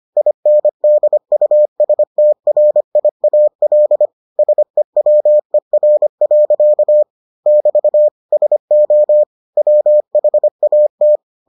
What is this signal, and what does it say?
Morse code, heard as a keyed tone, about 25 wpm, 600 Hz. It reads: INDUSTRIAL SEWER. <BT> SO WHAT